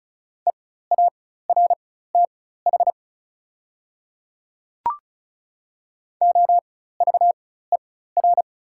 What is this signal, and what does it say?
Morse code, heard as a keyed tone, about 36 wpm, spaced as for 16 wpm, 700 Hz. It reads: EARTH OVER
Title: 4x Character Spacing - Top 500 Words - 35wpm